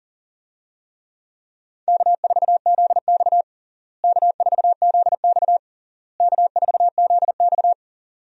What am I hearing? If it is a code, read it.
Morse code, heard as a keyed tone, about 37 wpm, 700 Hz. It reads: K4ZX K4ZX K4ZX
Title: USA Call Signs - 4 Characters - Repeated 3x - 40wpm